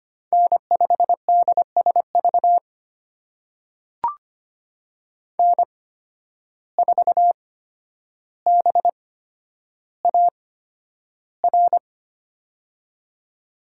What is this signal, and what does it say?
Morse code, heard as a keyed tone, about 25 wpm, spaced as for 11 wpm, 700 Hz. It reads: N5DSV N4BAR
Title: USA Call Signs - 5 Characters - 8x Character Spacing - 25wpm